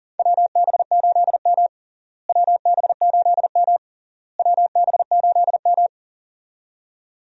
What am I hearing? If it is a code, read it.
Morse code, heard as a keyed tone, about 40 wpm, 700 Hz. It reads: WB8M WB8M WB8M
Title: USA Call Signs - 4 Characters - Repeated 3x - 40wpm